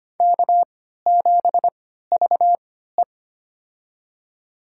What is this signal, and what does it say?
Morse code, heard as a keyed tone, about 25 wpm, spaced as for 14 wpm, 700 Hz. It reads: K7VE
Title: USA Call Signs - 3x Character Spacing - 25wpm